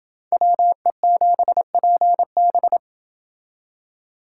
Morse code, heard as a keyed tone, about 29 wpm, 700 Hz. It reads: WE7PB